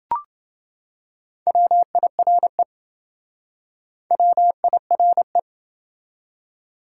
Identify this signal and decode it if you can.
Morse code, heard as a keyed tone, about 28 wpm, 700 Hz. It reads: WIRE WIRE